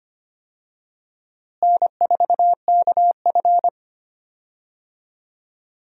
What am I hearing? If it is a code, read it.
Morse code, heard as a keyed tone, about 25 wpm, 700 Hz. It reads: N4KF
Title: USA Call Signs - 4 Characters - Repeated 2x - 25wpm